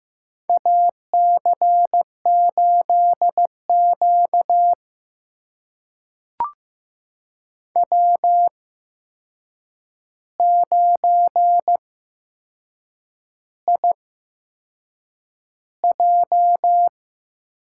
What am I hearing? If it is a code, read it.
Morse code, heard as a keyed tone, about 15 wpm, spaced as for 6 wpm, 700 Hz. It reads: AC8Q W9IJ
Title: USA Call Signs - 4 Characters - 8x Character Spacing - 15wpm